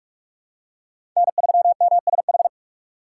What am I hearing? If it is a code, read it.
Morse code, heard as a keyed tone, about 46 wpm, 700 Hz. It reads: N3MSH
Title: USA Call Signs - 5 Characters - 6x Character Spacing - 45wpm